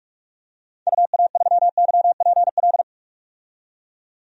Morse code, heard as a keyed tone, about 46 wpm, 700 Hz. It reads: UA3YPL
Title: DXCC-100 Call Signs - 6 Characters - 45wpm